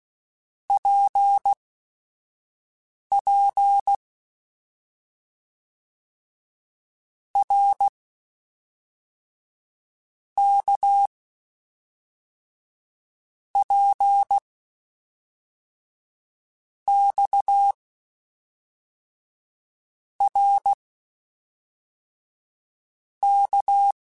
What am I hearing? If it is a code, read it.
Morse code, heard as a keyed tone, about 16 wpm, spaced as for 5 wpm, 800 Hz. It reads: PP R K P X R K
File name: Basic Sets / Basic Letter Groups Collective7 / RKPX@5